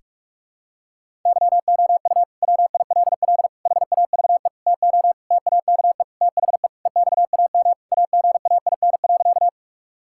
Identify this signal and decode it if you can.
Morse code, heard as a keyed tone, about 46 wpm, 700 Hz. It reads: YOU WILL HAVE TO TAKE THE EXAM AGAIN.
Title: Sentences from Top 1000 Words - Review Part 5 - 45wpm